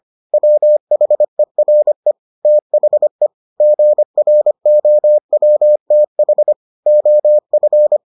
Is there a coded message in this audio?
Yes: Morse code from 0.3 to 8.0 s, heard as a keyed tone, about 25 wpm, 600 Hz. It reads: WHERE THE GROWTH OF